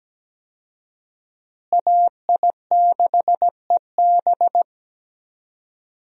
Morse code, heard as a keyed tone, about 17 wpm, 700 Hz. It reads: AI6EB
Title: USA Call Signs - 5 Characters - Repeated 3x - 17wpm